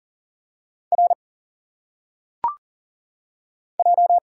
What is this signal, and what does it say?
Morse code, heard as a keyed tone, about 39 wpm, 700 Hz. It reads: R J
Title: Single Letters - Phonetic Pronunciation 40wpm